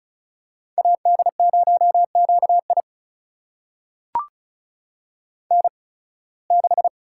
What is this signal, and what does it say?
Morse code, heard as a keyed tone, about 35 wpm, spaced as for 23 wpm, 690 Hz. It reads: AD0QI N6
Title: USA Call Signs - 5 Characters - 8x Character Spacing - 35wpm